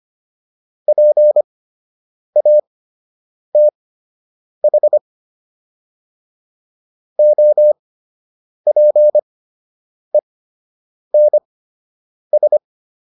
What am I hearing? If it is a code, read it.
Morse code, heard as a keyed tone, about 25 wpm, spaced as for 8 wpm, 600 Hz. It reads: PATH OPENS